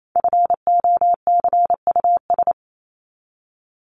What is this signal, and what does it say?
Morse code, heard as a keyed tone, about 28 wpm, 700 Hz. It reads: FOCUS
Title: Top 501-750 Words 28wpm